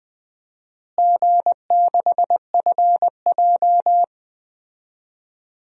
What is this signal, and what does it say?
Morse code, heard as a keyed tone, about 20 wpm, 700 Hz. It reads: G6FJ
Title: DXCC-100 Call Signs - 4 Characters - Repeated 3x - 20wpm